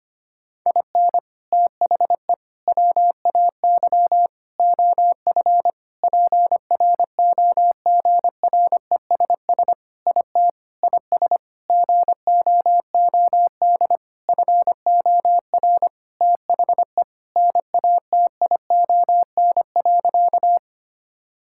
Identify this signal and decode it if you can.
Morse code, heard as a keyed tone, about 25 wpm, 700 Hz. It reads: IN THE WAY OF PROGRESS IT IS GOOD FOR THE NATION.
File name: practice / 25wpm / Sentences with 4-Word Phrases - 25wpm